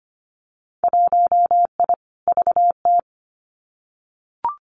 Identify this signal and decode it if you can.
Morse code, heard as a keyed tone, about 25 wpm, 700 Hz. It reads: 1I VT
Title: Winter Field Day Exchanges Encore 25wpm